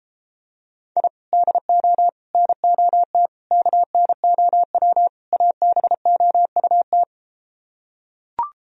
Morse code, heard as a keyed tone, about 33 wpm, 700 Hz. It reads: I DO NOT KNOW ABOUT